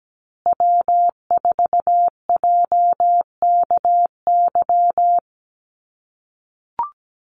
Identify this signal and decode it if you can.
Morse code, heard as a keyed tone, about 17 wpm, 700 Hz. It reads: W4JKY
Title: USA Call Signs Repeated 2x 17wpm